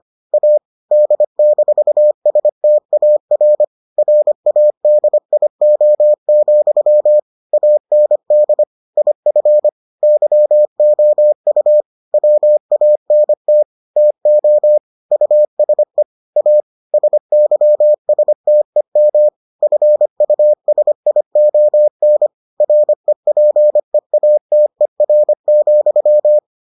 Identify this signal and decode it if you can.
Morse code, heard as a keyed tone, about 25 wpm, 600 Hz. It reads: A D-STAR RADIO, AND IF YOU WANT TO USE A SYSTEM FUSION REPEATER,